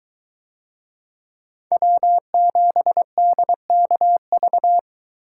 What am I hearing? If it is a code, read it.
Morse code, heard as a keyed tone, about 21 wpm, 700 Hz. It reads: W7DKV